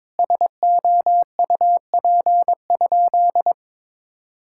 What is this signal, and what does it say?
Morse code, heard as a keyed tone, about 21 wpm, 700 Hz. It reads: SOUP?
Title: Sentences from Top 1000 Words - Review Part 6 - 22wpm